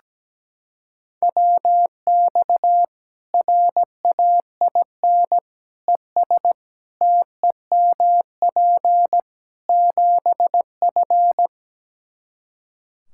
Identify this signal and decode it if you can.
Morse code, heard as a keyed tone, about 17 wpm, 700 Hz. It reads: WX RAIN ES TEMP 7F
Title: Practice QSO - Exchange 2 - Info - 17wpm